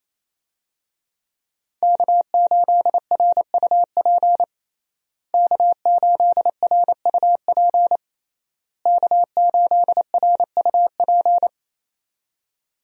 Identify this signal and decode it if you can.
Morse code, heard as a keyed tone, about 28 wpm, 700 Hz. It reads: K8RUP K8RUP K8RUP
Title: USA Call Signs - 5 Characters - Repeated 3x - 28wpm